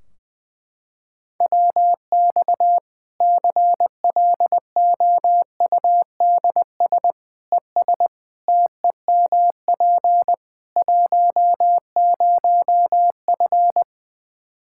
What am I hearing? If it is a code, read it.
Morse code, heard as a keyed tone, about 21 wpm, 700 Hz. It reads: WX CLOUDS ES TEMP 10F
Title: Practice QSO - Exchange 2 - Info - 20wpm